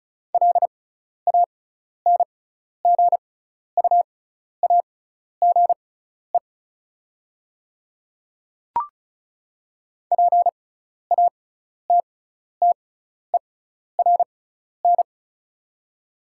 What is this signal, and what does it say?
Morse code, heard as a keyed tone, about 34 wpm, spaced as for 12 wpm, 690 Hz. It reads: LANGUAGE PATTERN
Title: 6x Character Spacing - Top 500 Words - 35wpm